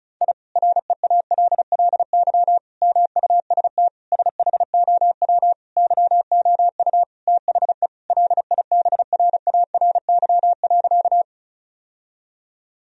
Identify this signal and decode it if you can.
Morse code, heard as a keyed tone, about 36 wpm, 700 Hz. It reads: I REALLY MUST SHOW YOU THE LIBRARY.
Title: Sentences from Top 1000 Words - Review Part 4 - 35wpm